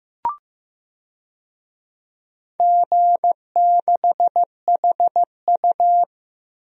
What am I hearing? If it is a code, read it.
Morse code, heard as a keyed tone, about 15 wpm, 700 Hz. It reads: G6HU
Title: DXCC-100 Call Signs - 4 Characters - Repeated 2x - 15wpm